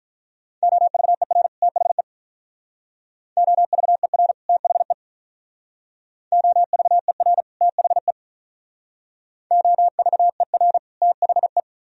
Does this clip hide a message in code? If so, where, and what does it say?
Yes: Morse code from 0.6 to 11.6 s, heard as a keyed tone, about 44 wpm, 700 Hz. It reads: OVER THE OVER THE OVER THE OVER THE